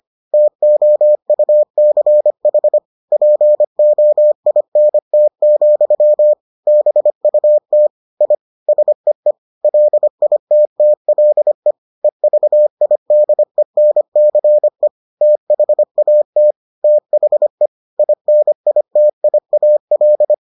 Morse code, heard as a keyed tone, about 25 wpm, 600 Hz. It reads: TOUCH POINT, BUT I SEE LITTLE EVIDENCE THAT THE INITIAL